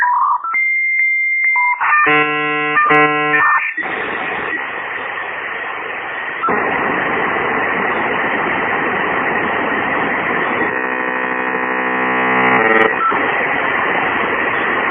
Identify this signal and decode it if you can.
Dial-up modem handshake.